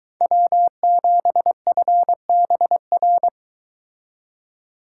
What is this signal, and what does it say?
Morse code, heard as a keyed tone, about 24 wpm, 700 Hz. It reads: W7FBR